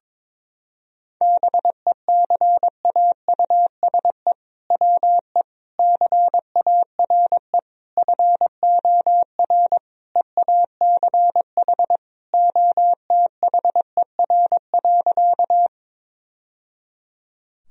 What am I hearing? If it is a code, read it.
Morse code, heard as a keyed tone, about 22 wpm, 700 Hz. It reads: BECAUSE WE CARE FOR EACH OTHER.
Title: Sentences from Top 1000 Words - Review Part 3 - 22wpm